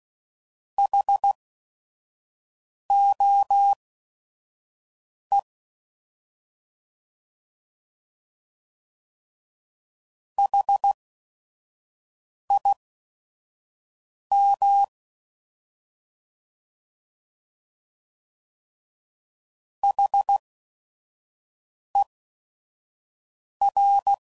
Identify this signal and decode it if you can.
Morse code, heard as a keyed tone, about 16 wpm, spaced as for 5 wpm, 800 Hz. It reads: HOE HIM HER